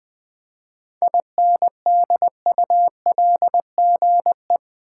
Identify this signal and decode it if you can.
Morse code, heard as a keyed tone, about 20 wpm, 700 Hz. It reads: INDULGE
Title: 7 Letter Words - First 3 Letters Given - 20wpm